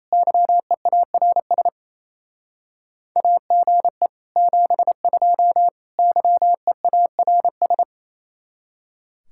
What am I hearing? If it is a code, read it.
Morse code, heard as a keyed tone, about 34 wpm, 700 Hz. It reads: YEARS AGE 72 YEARS
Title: Speed-Racing Common QSO Phrases - 28wpm